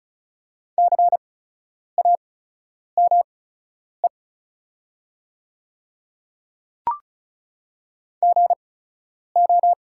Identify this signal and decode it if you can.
Morse code, heard as a keyed tone, about 36 wpm, spaced as for 10 wpm, 690 Hz. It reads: CAME GO
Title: 8x Character Spacing - Top 200 Words - 35wpm